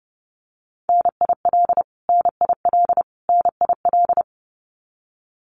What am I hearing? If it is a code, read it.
Morse code, heard as a keyed tone, about 30 wpm, 700 Hz. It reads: NIL NIL NIL
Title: Abbreviations Repeated 3x - 30wpm